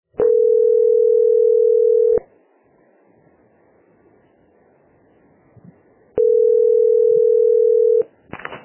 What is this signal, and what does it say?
Ringback tone (North America), 440 + 480 Hz.